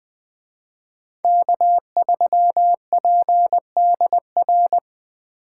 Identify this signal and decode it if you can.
Morse code, heard as a keyed tone, about 20 wpm, 700 Hz. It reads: K3PDR
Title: Speed Racing - USA 1x3 Call Signs 17wpm